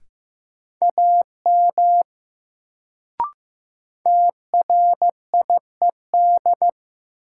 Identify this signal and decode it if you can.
Morse code, heard as a keyed tone, about 15 wpm, 700 Hz. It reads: AM TRIED
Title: Sets of 2 Words - New Word Drill - 15wpm